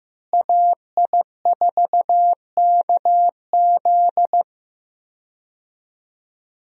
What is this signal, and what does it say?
Morse code, heard as a keyed tone, about 16 wpm, 700 Hz. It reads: AI4KZ
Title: USA Call Signs - 5 Characters - Repeated 3x - 15wpm